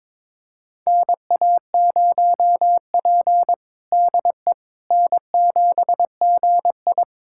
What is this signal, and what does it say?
Morse code, heard as a keyed tone, about 22 wpm, 700 Hz. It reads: NA0P DE N7GI
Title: USA Callsign DE Callsign - 4 Characters - 22wpm